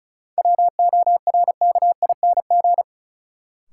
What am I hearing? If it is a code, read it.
Morse code, heard as a keyed tone, about 36 wpm, 700 Hz. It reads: WORKING
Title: Sentences from Top 700 Words - New Word Drill - 35wpm